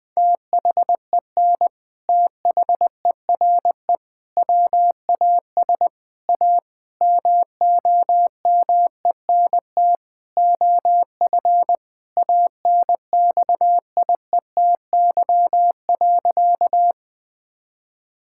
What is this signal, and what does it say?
Morse code, heard as a keyed tone, about 20 wpm, 700 Hz. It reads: THEN THERE WAS A MOMENT OF ANXIETY.
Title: Sentences from Top 1600 Words 20wpm